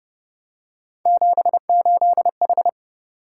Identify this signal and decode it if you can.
Morse code, heard as a keyed tone, about 29 wpm, 700 Hz. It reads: 78H